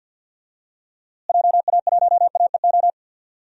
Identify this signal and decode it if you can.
Morse code, heard as a keyed tone, about 53 wpm, 700 Hz. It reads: JA1AEO
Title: DXCC-100 Call Signs - 6 Characters - Repeated 2x - 50wpm